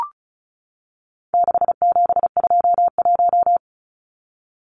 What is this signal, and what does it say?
Morse code, heard as a keyed tone, about 34 wpm, 700 Hz. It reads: 6721
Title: Numbers - 4-Digit 35wpm